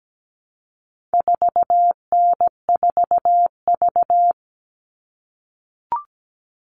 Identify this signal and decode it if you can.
Morse code, heard as a keyed tone, about 17 wpm, 700 Hz. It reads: 4N4V
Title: All Cut Numbers - 4-Digit 17wpm